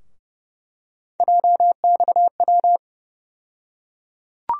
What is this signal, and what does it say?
Morse code, heard as a keyed tone, about 31 wpm, 700 Hz. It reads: JXW
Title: USA Call Sign Postfixes - 3 Letters 30wpm